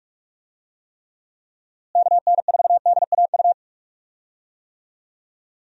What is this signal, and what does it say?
Morse code, heard as a keyed tone, about 46 wpm, 700 Hz. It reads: KN4DAU
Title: USA Call Signs - 6 Characters - Repeated 2x - 45wpm